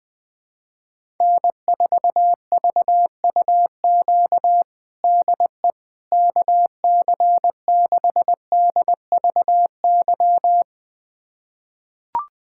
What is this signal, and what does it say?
Morse code, heard as a keyed tone, about 20 wpm, 700 Hz. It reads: N4VUQ DE KC6DVY
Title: USA Callsign DE Callsign 20wpm